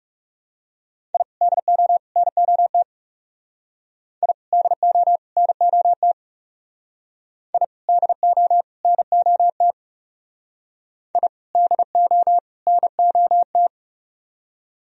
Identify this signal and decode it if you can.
Morse code, heard as a keyed tone, about 37 wpm, 700 Hz. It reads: I DO NOT I DO NOT I DO NOT I DO NOT
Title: Speed-Racing - Sets of 3 Words 30wpm